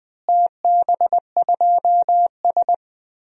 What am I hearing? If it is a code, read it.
Morse code, heard as a keyed tone, about 21 wpm, 700 Hz. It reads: TB2S
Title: DXCC-100 Call Signs - 4 Characters - Repeated 3x - 20wpm